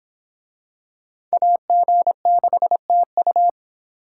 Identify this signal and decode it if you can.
Morse code, heard as a keyed tone, about 25 wpm, 700 Hz. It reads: AG6TU